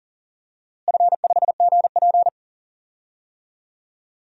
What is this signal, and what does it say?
Morse code, heard as a keyed tone, about 40 wpm, 700 Hz. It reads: F5GP
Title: DXCC-100 Call Signs - 4 Characters - Repeated 2x - 40wpm